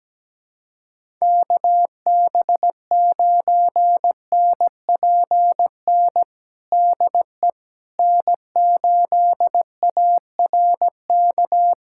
Morse code, heard as a keyed tone, about 17 wpm, 700 Hz. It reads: KB9NPN DE N8ARK
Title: USA Callsign DE Callsign 17wpm